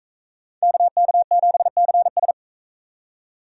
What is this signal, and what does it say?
Morse code, heard as a keyed tone, about 44 wpm, 700 Hz. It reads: KK7CS